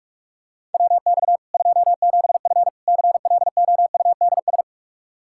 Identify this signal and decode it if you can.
Morse code, heard as a keyed tone, about 46 wpm, 700 Hz. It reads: WX 27F CLOUDS
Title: K1USN SST QSO Practice 45wpm